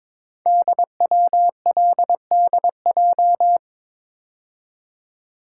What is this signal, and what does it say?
Morse code, heard as a keyed tone, about 22 wpm, 700 Hz. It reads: DWLDJ
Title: Code Groups - 5 Letters - 22wpm